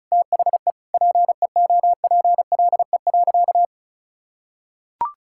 Morse code, heard as a keyed tone, about 34 wpm, 700 Hz. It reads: THE PEOPLE.
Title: Sentences from Top 1000 Words - Review Part 6 - 35wpm